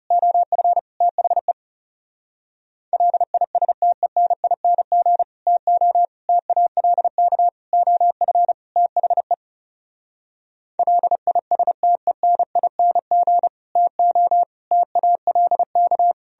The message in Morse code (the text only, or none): OF THE LISTENING TO TALK OF THE LISTENING TO TALK